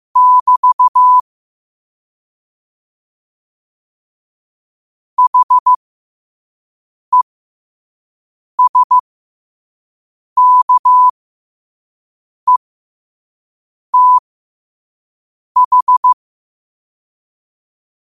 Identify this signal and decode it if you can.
Morse code, heard as a keyed tone, about 15 wpm, spaced as for 5 wpm, 1000 Hz. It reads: <BT> HESKETH
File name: qotdfiles / 5wpm